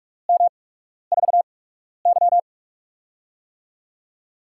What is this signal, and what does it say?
Morse code, heard as a keyed tone, about 44 wpm, 700 Hz. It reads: M 4 Y
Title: USA Call Signs - 8x Character Spacing - 45wpm